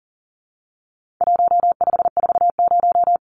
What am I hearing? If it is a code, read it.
Morse code, heard as a keyed tone, about 40 wpm, 700 Hz. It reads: 1540
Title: Numbers - 4-Digit 40wpm